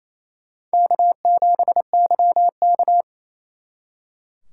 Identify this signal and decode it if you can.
Morse code, heard as a keyed tone, about 35 wpm, 700 Hz. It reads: K7YK
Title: Speed Racing - USA 1x2 Call Signs 28wpm